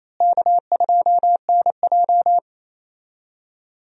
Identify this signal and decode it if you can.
Morse code, heard as a keyed tone, about 28 wpm, 700 Hz. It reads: K2NJ
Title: USA Call Signs - 4 Characters - Repeated 2x - 28wpm